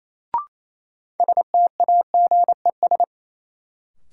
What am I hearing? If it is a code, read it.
Morse code, heard as a keyed tone, about 28 wpm, 700 Hz. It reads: STAGES
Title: Sets of 5 Words Encore - New Word Drill - 28wpm